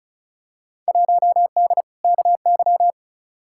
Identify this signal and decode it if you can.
Morse code, heard as a keyed tone, about 34 wpm, 700 Hz. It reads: 1D KY
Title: Field Day Exchanges - 35wpm